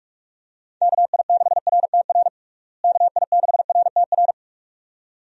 Morse code, heard as a keyed tone, about 46 wpm, 700 Hz. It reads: KI6RTR KI6RTR
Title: USA Call Signs - 6 Characters - Repeated 2x - 45wpm